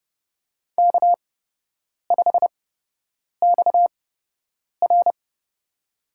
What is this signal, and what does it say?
Morse code, heard as a keyed tone, about 30 wpm, spaced as for 8 wpm, 700 Hz. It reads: K5XR
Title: USA Call Signs - 4 Characters - 8x Character Spacing - 30wpm